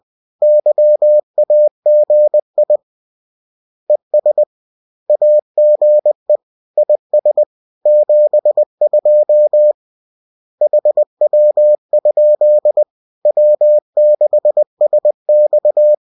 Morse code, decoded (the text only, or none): YAGI ES AGE IS 72 HW? W6SX